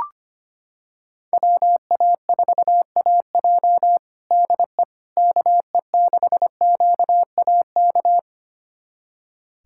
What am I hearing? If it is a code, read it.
Morse code, heard as a keyed tone, about 25 wpm, 700 Hz. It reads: WA4AJ DE KE6QAK
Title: USA Callsign DE Callsign 25wpm